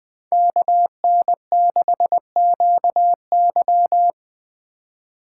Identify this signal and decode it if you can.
Morse code, heard as a keyed tone, about 24 wpm, 700 Hz. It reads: KN6QY
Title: Speed Racing - USA 2x2 Call Signs 20wpm